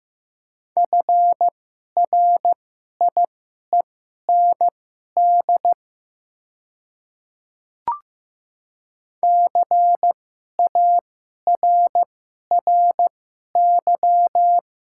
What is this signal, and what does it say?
Morse code, heard as a keyed tone, about 15 wpm, spaced as for 11 wpm, 700 Hz. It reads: FRIEND CARRY
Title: 2x Character Spacing - Top 500 Words - 15wpm